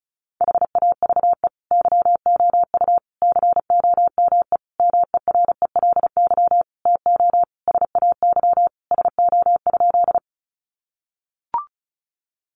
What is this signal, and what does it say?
Morse code, heard as a keyed tone, about 34 wpm, 700 Hz. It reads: HAVE YOU COME MERELY TO SAY SO?
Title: Sentences from Top 1400 Words 35wpm